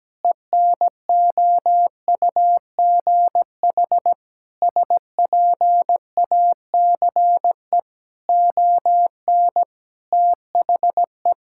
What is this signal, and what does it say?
Morse code, heard as a keyed tone, about 17 wpm, 700 Hz. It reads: ENOUGH SPACE ON THE
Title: Sentences from Top 1000 Words - Review Part 6 - 17wpm